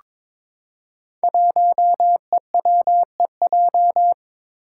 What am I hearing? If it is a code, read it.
Morse code, heard as a keyed tone, about 22 wpm, 700 Hz. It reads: 1EWEJ
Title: Five Challenging Letter-Number Combinations 22wpm